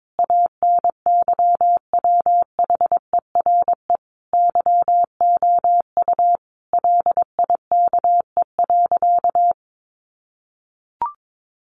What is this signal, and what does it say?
Morse code, heard as a keyed tone, about 22 wpm, 700 Hz. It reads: ANYWHERE YOU LIKE.
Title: Sentences from Top 1400 Words 22wpm